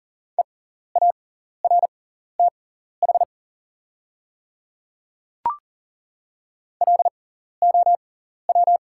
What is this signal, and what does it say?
Morse code, heard as a keyed tone, about 39 wpm, spaced as for 14 wpm, 690 Hz. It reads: EARTH LOW
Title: 6x Character Spacing - Top 200 Words - 40wpm